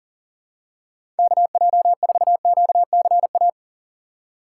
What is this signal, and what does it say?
Morse code, heard as a keyed tone, about 40 wpm, 700 Hz. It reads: KJ4QCA
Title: USA Call Signs - 4x Character Spacing - 40wpm